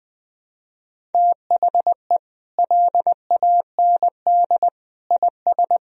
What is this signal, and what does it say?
Morse code, heard as a keyed tone, about 20 wpm, 700 Hz. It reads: THE LAND IS
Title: Sentences from Top 1000 Words - Review Part 2 - 20wpm